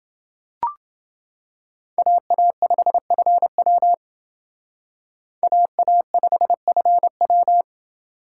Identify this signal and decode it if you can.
Morse code, heard as a keyed tone, about 28 wpm, 700 Hz. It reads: AA5FW AA5FW